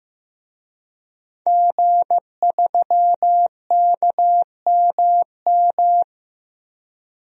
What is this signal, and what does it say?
Morse code, heard as a keyed tone, about 16 wpm, 700 Hz. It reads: G3KMM
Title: DXCC-100 Call Signs - 5 Characters - Repeated 3x - 15wpm